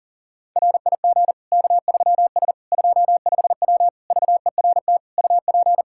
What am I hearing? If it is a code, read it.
Morse code, heard as a keyed tone, about 37 wpm, 700 Hz. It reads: RIG K3S 25W VERT UP
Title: Practice QSO - Exchange 2 - Info - 40wpm